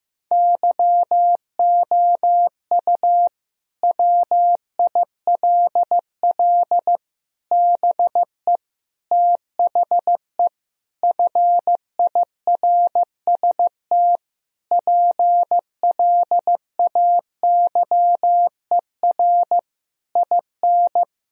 Morse code, heard as a keyed tone, about 15 wpm, 700 Hz. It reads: YOU WILL BE THE FIRST PLAYER IN
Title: Sentences from Top 1000 Words - Review Part 4 - 15wpm